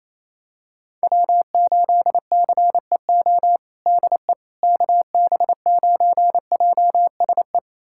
Morse code, heard as a keyed tone, about 27 wpm, 700 Hz. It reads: W8CEO DE KB9JSE
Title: USA Callsign DE Callsign 28wpm